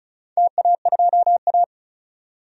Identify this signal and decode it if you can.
Morse code, heard as a keyed tone, about 36 wpm, 700 Hz. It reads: TA2A
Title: DXCC-100 Call Signs - 4 Characters - 35wpm